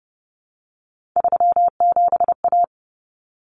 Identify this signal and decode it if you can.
Morse code, heard as a keyed tone, about 30 wpm, 700 Hz. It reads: 37A
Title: Common Cut Numbers - 3-Digit 30wpm